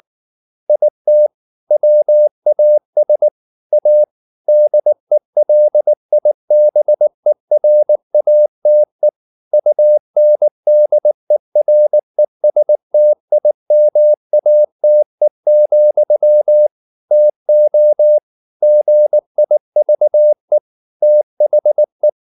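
Morse code, heard as a keyed tone, about 19 wpm, 600 Hz. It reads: IT WAS A DELIBERATE UNDERESTIMATE, TO GIVE THE
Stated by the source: Morse code at 19 wpm from CW med Gnister, reading Ken Follett's novel Never in English.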